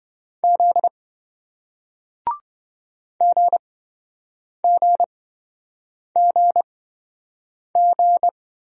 Morse code, heard as a keyed tone, about 28 wpm, spaced as for 7 wpm, 700 Hz. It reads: Z GGGG